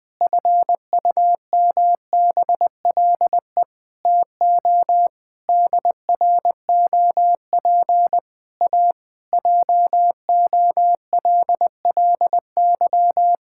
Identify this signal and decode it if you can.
Morse code, heard as a keyed tone, about 20 wpm, 700 Hz. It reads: FUMBLE TO DROP A JOLLY
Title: Pangram Challenge v2 - 20wpm